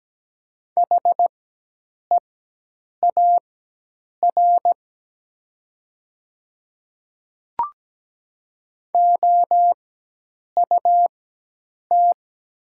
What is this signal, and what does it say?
Morse code, heard as a keyed tone, about 17 wpm, spaced as for 8 wpm, 700 Hz. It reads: HEAR OUT
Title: 4x Character Spacing - Top 300 Words - 17wpm